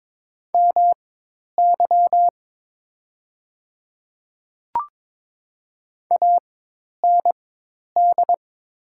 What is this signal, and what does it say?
Morse code, heard as a keyed tone, about 22 wpm, spaced as for 10 wpm, 690 Hz. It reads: MY AND
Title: 4x Character Spacing - Top 200 Words - 22wpm